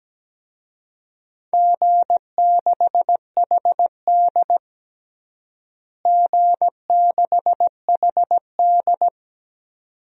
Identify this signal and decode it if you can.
Morse code, heard as a keyed tone, about 17 wpm, 700 Hz. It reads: G6HD G6HD
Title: DXCC-100 Call Signs - 4 Characters - Repeated 2x - 17wpm